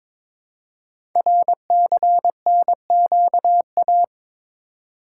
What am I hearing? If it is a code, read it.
Morse code, heard as a keyed tone, about 22 wpm, 700 Hz. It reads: RCNQA
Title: Code Groups - 5 Letters - 22wpm